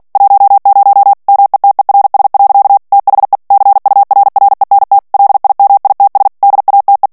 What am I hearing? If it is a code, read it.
Morse code, heard as a keyed tone, about 48 wpm, 800 Hz. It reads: 10 METERS. THE CURRENT LIMITS DATE
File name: ANGLAIS texte clair 3 vitesse 45 mots minute